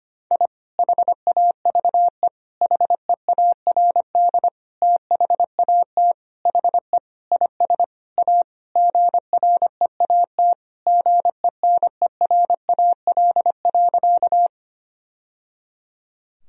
Morse code, heard as a keyed tone, about 25 wpm, 700 Hz. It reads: I HAVE HEARD THAT HE IS A GREAT GENERAL.
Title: Sentences from Top 1000 Words - Review Part 3 - 25wpm